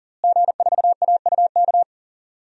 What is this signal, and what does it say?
Morse code, heard as a keyed tone, about 38 wpm, 700 Hz. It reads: G4AUK
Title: DXCC-100 Call Signs - 5 Characters - Repeated 3x - 40wpm